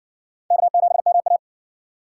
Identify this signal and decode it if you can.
Morse code, heard as a keyed tone, about 52 wpm, 700 Hz. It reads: K7MA